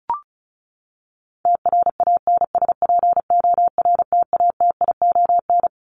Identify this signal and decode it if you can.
Morse code, heard as a keyed tone, about 34 wpm, 700 Hz. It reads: TRANSPORTATION
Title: Top 1501-1750 Words 35wpm